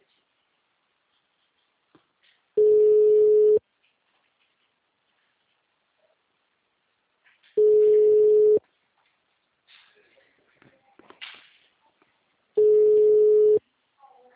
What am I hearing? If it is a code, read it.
Ringback tone (Europe, 425 Hz), with speech.